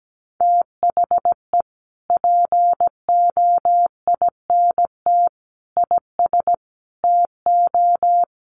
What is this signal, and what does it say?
Morse code, heard as a keyed tone, about 17 wpm, 700 Hz. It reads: THE POINT IS TO
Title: Sentences from Top 1300 Words 17wpm